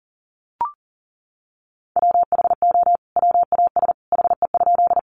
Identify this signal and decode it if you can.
Morse code, heard as a keyed tone, about 40 wpm, 700 Hz. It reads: WHO WAS HE?
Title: Sentences from Top 100 Words 40wpm